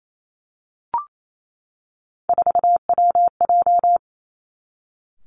Morse code, heard as a keyed tone, about 28 wpm, 700 Hz. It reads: 4WJ
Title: Three Challenging Letter-Number Combinations 28wpm